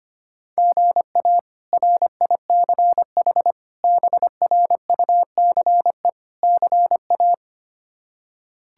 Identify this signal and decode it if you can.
Morse code, heard as a keyed tone, about 25 wpm, 700 Hz. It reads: GA RICH BRUCE CA
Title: K1USN SST QSO Practice - 25wpm